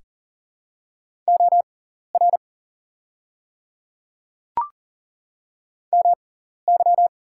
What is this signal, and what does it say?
Morse code, heard as a keyed tone, about 40 wpm, spaced as for 14 wpm, 700 Hz. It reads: OR MY
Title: 6x Character Spacing - Top 500 Words - 40wpm